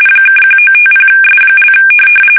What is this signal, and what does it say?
RTTY, about 45.45 baud. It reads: (1950S)
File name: fsk850 / teletype-corp